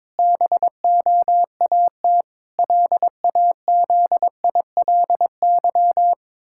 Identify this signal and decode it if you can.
Morse code, heard as a keyed tone, about 22 wpm, 700 Hz. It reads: BOAT LAZILY
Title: Pangram Challenge v2 - 22wpm